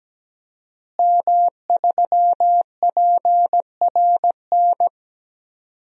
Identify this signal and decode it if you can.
Morse code, heard as a keyed tone, about 17 wpm, 700 Hz. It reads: M3PRN
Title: DXCC-100 Call Signs - 5 Characters - Repeated 2x - 17wpm